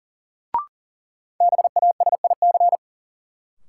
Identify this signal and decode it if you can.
Morse code, heard as a keyed tone, about 37 wpm, 700 Hz. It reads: BASIC
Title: Sentences from Top 1000 Words - Review Part 3 - New Word Drill - 40wpm